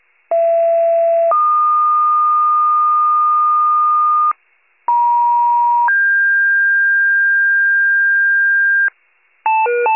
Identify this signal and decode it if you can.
Two-tone page, heard as a one-second tone, then a three-second tone, about 670 then 1220 Hz, about 950 then 1600 Hz, with noise. It is heard from 0.3 to 8.9 s.
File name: wcalert